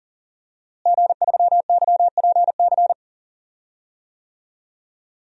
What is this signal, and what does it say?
Morse code, heard as a keyed tone, about 40 wpm, 700 Hz. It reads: G3YPC
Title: DXCC-100 Call Signs - 5 Characters - Repeated 2x - 40wpm